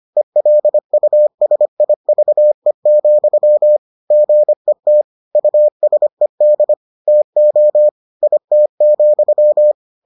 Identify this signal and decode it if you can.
Morse code, heard as a keyed tone, about 25 wpm, 600 Hz. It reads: ELUSIVE, GET USED TO IT,